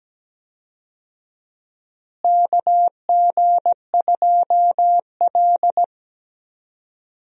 Morse code, heard as a keyed tone, about 17 wpm, 700 Hz. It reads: KG2L
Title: USA Call Signs - 4 Characters - Repeated 2x - 17wpm